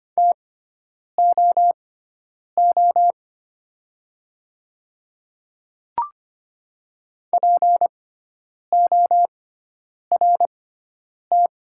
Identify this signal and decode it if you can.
Morse code, heard as a keyed tone, about 25 wpm, spaced as for 9 wpm, 690 Hz. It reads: TOO PORT
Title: 6x Character Spacing - Top 200 Words - 25wpm